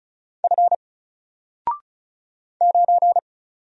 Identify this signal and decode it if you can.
Morse code, heard as a keyed tone, about 35 wpm, 700 Hz. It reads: F 9
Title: Single Character (Letter:Number:Punctuation:Prosign) 35wpm